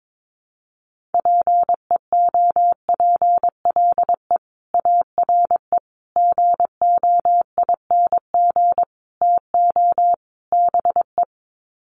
Morse code, heard as a keyed tone, about 22 wpm, 700 Hz. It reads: PEOPLE ARE GOING TO BE
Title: Sets of 5 Words Encore 22wpm